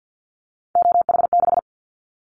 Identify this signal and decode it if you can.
Morse code, heard as a keyed tone, about 50 wpm, 700 Hz. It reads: OH6
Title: Callsign Prefixes for Top 25 DXCC - Repeated 3x - 50wpm